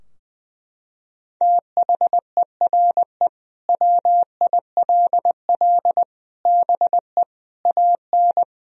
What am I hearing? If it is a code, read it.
Morse code, heard as a keyed tone, about 21 wpm, 700 Hz. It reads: THERE WILL BE AN
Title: Sentences from Top 1000 Words - Review Part 4 - 20wpm